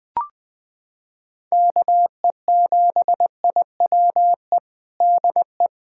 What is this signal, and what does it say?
Morse code, heard as a keyed tone, about 20 wpm, 700 Hz. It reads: KE7IWE DE
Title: USA Callsign DE Callsign 20wpm